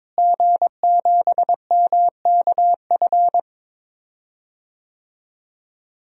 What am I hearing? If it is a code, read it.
Morse code, heard as a keyed tone, about 22 wpm, 700 Hz. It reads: G7MKF
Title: DXCC-100 Call Signs - 5 Characters - Repeated 2x - 22wpm